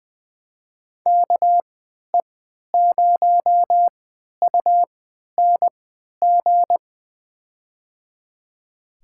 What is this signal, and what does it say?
Morse code, heard as a keyed tone, about 20 wpm, spaced as for 11 wpm, 700 Hz. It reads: KE0UNG
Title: USA Call Signs - 6 Characters - 3x Character Spacing - 20wpm